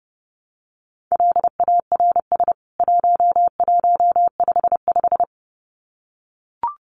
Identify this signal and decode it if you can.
Morse code, heard as a keyed tone, about 30 wpm, 700 Hz. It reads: LARS 1155
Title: CWT Exchange 30wpm